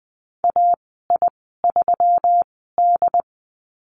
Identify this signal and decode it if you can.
Morse code, heard as a keyed tone, about 20 wpm, spaced as for 15 wpm, 700 Hz. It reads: AI3D
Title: 147 Lesson 24g - Letter y - Call Signs 20wpm